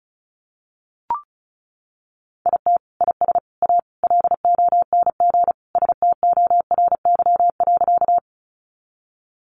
Morse code, heard as a keyed tone, about 34 wpm, 700 Hz. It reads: IT IS A LONG STORY.
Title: Sentences from Top 300 Words 35wpm